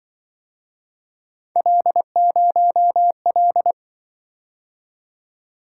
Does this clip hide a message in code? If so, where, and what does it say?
Yes: Morse code from 1.6 to 3.7 s, heard as a keyed tone, about 25 wpm, 700 Hz. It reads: L0L